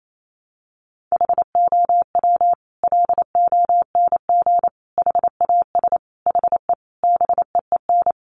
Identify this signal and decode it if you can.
Morse code, heard as a keyed tone, about 28 wpm, 700 Hz. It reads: HOW LONG HAS HE BEEN
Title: Sentences from Top 100 Words 28wpm